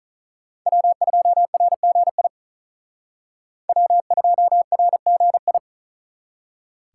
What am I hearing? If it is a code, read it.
Morse code, heard as a keyed tone, about 46 wpm, 700 Hz. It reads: W2RGI W2RGI